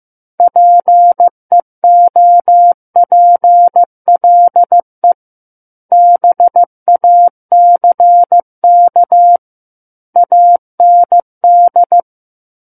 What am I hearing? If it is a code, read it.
Morse code, heard as a keyed tone, about 15 wpm, 700 Hz. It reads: PEOPLE BACK AND
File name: Book One / 06-Chapter 6 - 15 WPM